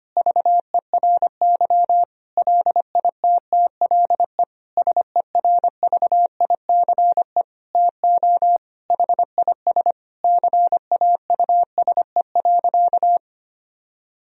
Morse code, heard as a keyed tone, about 25 wpm, 700 Hz. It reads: VERY LITTLE SERVICE TO HIS CAUSE.
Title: Sentences from Top 1000 Words - Review Part 4 - 25wpm